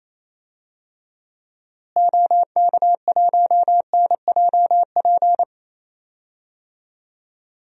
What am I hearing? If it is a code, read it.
Morse code, heard as a keyed tone, about 28 wpm, 700 Hz. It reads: OK1NJP
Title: DXCC-100 Call Signs - 6 Characters - Repeated 3x - 28wpm